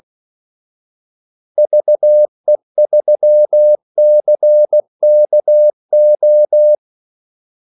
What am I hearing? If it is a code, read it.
Morse code, heard as a keyed tone, about 16 wpm, 600 Hz. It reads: VE3CKO
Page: Kaldesignaler 16 wpm